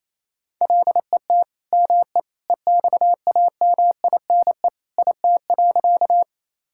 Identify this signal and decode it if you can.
Morse code, heard as a keyed tone, about 28 wpm, 700 Hz. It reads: LET ME EXAMINE IT.
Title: Sentences from Top 1600 Words 28wpm